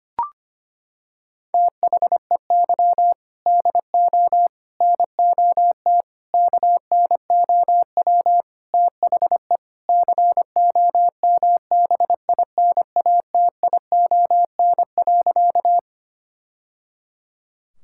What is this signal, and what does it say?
Morse code, heard as a keyed tone, about 25 wpm, 700 Hz. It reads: THEY DO NOT KNOW THE COMBINATION.
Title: Sentences from Top 1000 Words - Review Part 6 - 25wpm